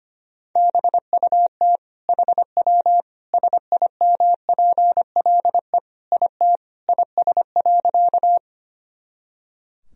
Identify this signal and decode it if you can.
Morse code, heard as a keyed tone, about 25 wpm, 700 Hz. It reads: BUT HW SIMPLE IT IS.
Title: Sentences from Top 400 Words and Abbreviations 25wpm